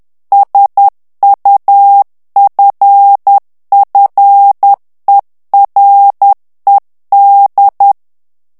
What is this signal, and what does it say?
Morse code, heard as a keyed tone, about 10 wpm, 800 Hz. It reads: SUFFERED
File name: ANGLAIS texte clair 2 vitesse 10 mots minute